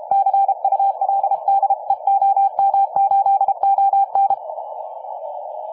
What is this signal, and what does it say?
Morse code, heard as a keyed tone, about 31 wpm, 810 Hz. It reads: CUL DE OM8ON